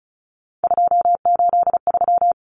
Morse code, heard as a keyed tone, about 34 wpm, 700 Hz. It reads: 283
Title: Common Cut Numbers - 3-Digit 35wpm